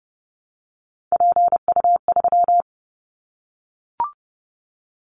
Morse code, heard as a keyed tone, about 30 wpm, 700 Hz. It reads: PU3
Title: Callsign Prefixes for Top 100 DXCC - 30wpm